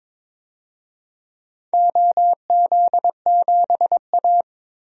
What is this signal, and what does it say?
Morse code, heard as a keyed tone, about 22 wpm, 700 Hz. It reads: OZ7A
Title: DXCC-100 Call Signs - 4 Characters - Repeated 3x - 22wpm